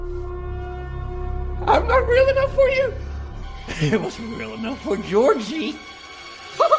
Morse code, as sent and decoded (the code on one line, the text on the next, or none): none